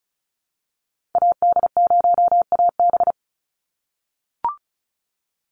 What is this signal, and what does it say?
Morse code, heard as a keyed tone, about 34 wpm, 700 Hz. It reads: AD0AB
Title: Call Signs - 35wpm